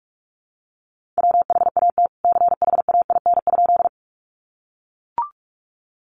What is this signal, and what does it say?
Morse code, heard as a keyed tone, about 44 wpm, 700 Hz. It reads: WHAT CHAIN?
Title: Sentences from Top 1500 Words 45wpm